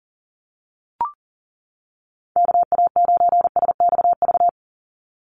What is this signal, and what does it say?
Morse code, heard as a keyed tone, about 40 wpm, 700 Hz. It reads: KA9SXV
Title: Call Signs - 40wpm